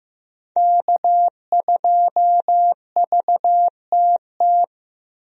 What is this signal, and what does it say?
Morse code, heard as a keyed tone, about 15 wpm, 700 Hz. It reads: K2VTT
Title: USA Call Signs - 5 Characters - Repeated 2x - 15wpm